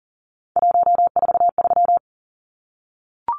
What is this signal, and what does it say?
Morse code, heard as a keyed tone, about 40 wpm, 700 Hz. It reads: 143
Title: Numbers - 3-Digit 40wpm